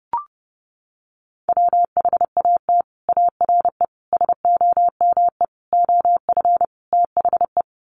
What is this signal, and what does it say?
Morse code, heard as a keyed tone, about 30 wpm, 700 Hz. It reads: WHAT ARE SOME OF THE
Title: Sets of 5 Words Encore 30wpm